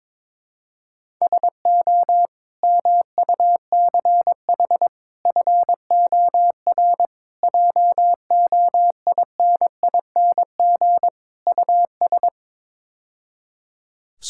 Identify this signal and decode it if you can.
Morse code, heard as a keyed tone, about 22 wpm, 700 Hz. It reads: SO MUCH FOR JOINING US
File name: practice / 22wpm / Build A Phrase - 5-Word Phrases - 22wpm